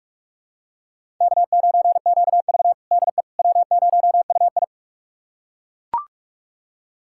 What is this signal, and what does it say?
Morse code, heard as a keyed tone, about 41 wpm, 700 Hz. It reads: K9QV DE W0UI
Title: USA Callsign DE Callsign - 4 Characters - 45wpm